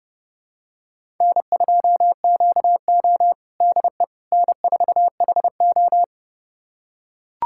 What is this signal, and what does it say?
Morse code, heard as a keyed tone, about 30 wpm, 700 Hz. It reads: N2QO DE N4HO
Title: USA Callsign DE Callsign - 4 Characters - 30wpm